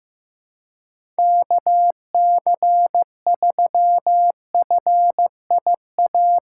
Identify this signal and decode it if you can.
Morse code, heard as a keyed tone, about 15 wpm, 700 Hz. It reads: KC3FIA
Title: USA Call Signs - 6 Characters - 4x Character Spacing - 15wpm